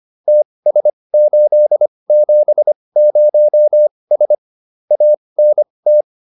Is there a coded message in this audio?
Yes: Morse code from 0.3 to 6.0 s, heard as a keyed tone, about 25 wpm, spaced as for 20 wpm, 600 Hz. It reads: TS870S ANT